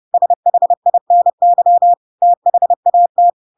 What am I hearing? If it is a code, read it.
Morse code, heard as a keyed tone, about 30 wpm, 700 Hz. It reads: SHINY THAT
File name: Pinocchio / Pinocchio ch1 30 30 ik1hge